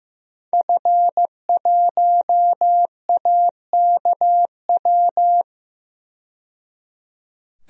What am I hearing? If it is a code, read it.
Morse code, heard as a keyed tone, about 15 wpm, 700 Hz. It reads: F1AKW
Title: DXCC-100 Call Signs - 5 Characters - 15wpm